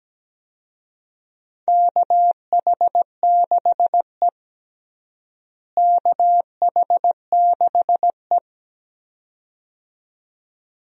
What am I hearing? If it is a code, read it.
Morse code, heard as a keyed tone, about 17 wpm, 700 Hz. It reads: KH6E KH6E
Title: DXCC-100 Call Signs - 4 Characters - Repeated 2x - 17wpm